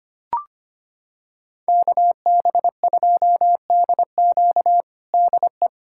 Morse code, heard as a keyed tone, about 25 wpm, 700 Hz. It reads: KB2DQ DE
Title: USA Callsign DE Callsign 25wpm